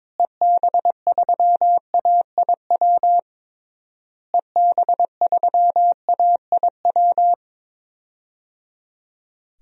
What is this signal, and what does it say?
Morse code, heard as a keyed tone, about 21 wpm, 700 Hz. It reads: EB3AIW EB3AIW
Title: DXCC-100 Call Signs - 6 Characters - Repeated 3x - 22wpm